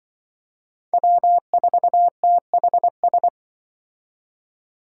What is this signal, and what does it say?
Morse code, heard as a keyed tone, about 24 wpm, 700 Hz. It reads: W4THS
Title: Speed Racing - USA 1x3 Call Signs 20wpm